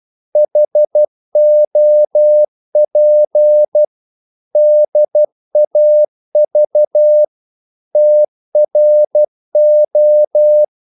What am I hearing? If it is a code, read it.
Morse code, heard as a keyed tone, about 12 wpm, 600 Hz. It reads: HOP DAV TRO
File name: Korte ord DK 12wpm